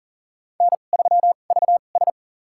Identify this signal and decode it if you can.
Morse code, heard as a keyed tone, about 40 wpm, spaced as for 29 wpm, 700 Hz. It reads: N3VS
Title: USA Call Signs - 4 Characters - 2x Character Spacing - 40wpm